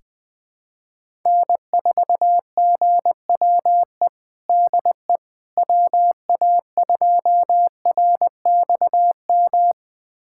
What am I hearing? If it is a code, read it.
Morse code, heard as a keyed tone, about 20 wpm, 700 Hz. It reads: N4GWE DE WA2RXM
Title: USA Callsign DE Callsign 20wpm